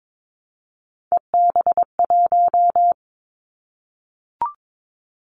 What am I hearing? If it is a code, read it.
Morse code, heard as a keyed tone, about 22 wpm, 700 Hz. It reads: EB1
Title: Callsign Prefixes for Top 25 DXCC - Repeated 3x - 22wpm